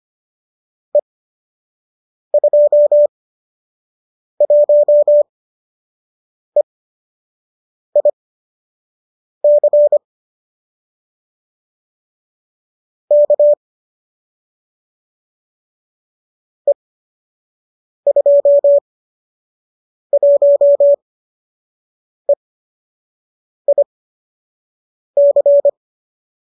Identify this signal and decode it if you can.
Morse code, heard as a keyed tone, about 25 wpm, spaced as for 6 wpm, 590 Hz. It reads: E21EIC K E21EIC